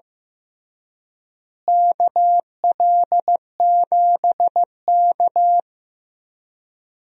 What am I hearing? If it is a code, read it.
Morse code, heard as a keyed tone, about 15 wpm, 700 Hz. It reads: KL7K
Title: DXCC-100 Call Signs - 4 Characters - Repeated 3x - 15wpm